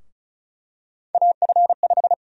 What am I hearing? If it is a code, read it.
Morse code, heard as a keyed tone, about 36 wpm, 700 Hz. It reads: AF5
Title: USA Call Sign Prefixes - 35wpm